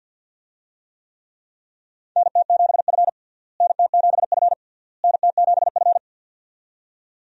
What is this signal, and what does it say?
Morse code, heard as a keyed tone, about 51 wpm, 700 Hz. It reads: NT7F NT7F NT7F
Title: USA Call Signs - 4 Characters - Repeated 3x - 50wpm